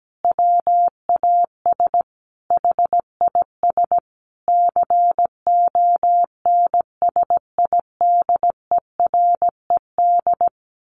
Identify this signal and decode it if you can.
Morse code, heard as a keyed tone, about 17 wpm, 700 Hz. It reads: WAS HIS CONSIDERED
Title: A Sound Of Thunder 17wpm